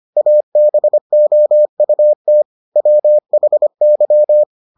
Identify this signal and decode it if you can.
Morse code, heard as a keyed tone, about 25 wpm, 600 Hz. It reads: ABOUT WHY